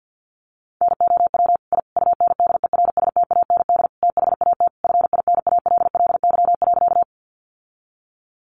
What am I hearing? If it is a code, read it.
Morse code, heard as a keyed tone, about 50 wpm, 700 Hz. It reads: NOW I UNDERSTAND THAT FINALLY.
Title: Sentences from Top 1000 Words 50wpm